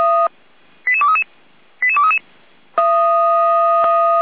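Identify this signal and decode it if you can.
Five-tone selective call (ZVEI), heard as quick runs of five tones, tones of about 80 ms. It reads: 88233 88233
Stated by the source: ZVEI five-tone call 88102 received on SDR.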